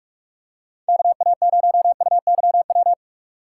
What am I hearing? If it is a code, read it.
Morse code, heard as a keyed tone, about 42 wpm, 700 Hz. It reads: KA0UYW